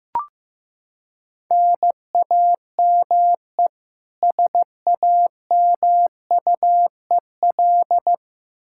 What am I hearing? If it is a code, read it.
Morse code, heard as a keyed tone, about 15 wpm, 700 Hz. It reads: NAME SAMUEL
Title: Practice QSO - Exchange 1 - Beginning - 15wpm